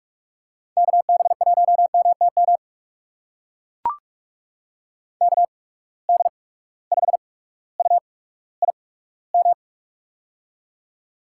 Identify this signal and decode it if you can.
Morse code, heard as a keyed tone, about 44 wpm, spaced as for 19 wpm, 700 Hz. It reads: KB1MTM KD5UIM
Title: USA Call Signs - 6 Characters - 8x Character Spacing - 45wpm